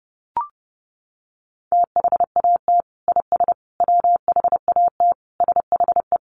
Morse code, heard as a keyed tone, about 30 wpm, 700 Hz. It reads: THAT IS WHAT SHE
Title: Sentences from Top 100 Words 30wpm